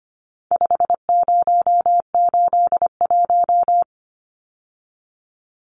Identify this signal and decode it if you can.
Morse code, heard as a keyed tone, about 25 wpm, 700 Hz. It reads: 5081
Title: Numbers - 4-Digit 25wpm